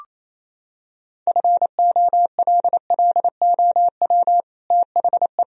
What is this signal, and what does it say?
Morse code, heard as a keyed tone, about 28 wpm, 700 Hz. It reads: FOLLOW THE
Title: Sentences from Top 1000 Words - Review Part 4 - 28wpm